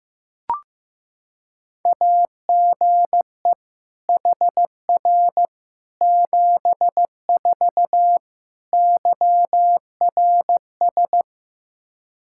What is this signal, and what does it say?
Morse code, heard as a keyed tone, about 15 wpm, 700 Hz. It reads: AGE HR 74 YRS
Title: Practice QSO - Exchange 3 - More Info - 15wpm